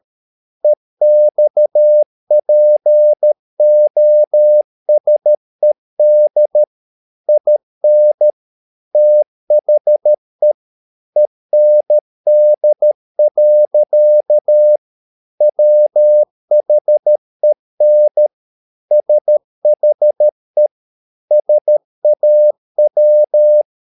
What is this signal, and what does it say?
Morse code, heard as a keyed tone, about 13 wpm, 600 Hz. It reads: EXPOSED IN THE END. WHEN SHE SAW
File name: mp3 files / Lexiphiles - 13wpm